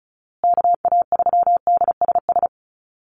Morse code, heard as a keyed tone, about 34 wpm, 700 Hz. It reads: KA3DSS
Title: USA Call Signs Repeated 3x 35wpm